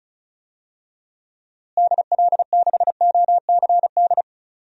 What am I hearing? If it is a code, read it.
Morse code, heard as a keyed tone, about 36 wpm, 700 Hz. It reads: DL6OCD
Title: DXCC-100 Call Signs - 6 Characters - Repeated 3x - 35wpm